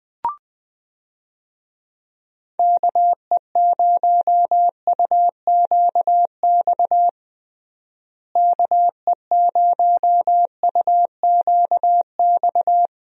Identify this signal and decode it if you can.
Morse code, heard as a keyed tone, about 20 wpm, 700 Hz. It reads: KE0UQX KE0UQX
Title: USA Call Signs - 6 Characters - Repeated 3x - 20wpm